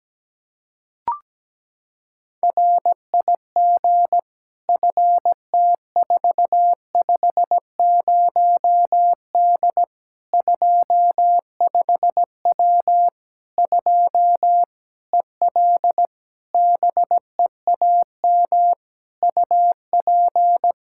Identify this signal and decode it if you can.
Morse code, heard as a keyed tone, about 17 wpm, 700 Hz. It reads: RIG FT450D 25W 2 EL BEAM UP
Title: Practice QSO - Exchange 2 - Info - 17wpm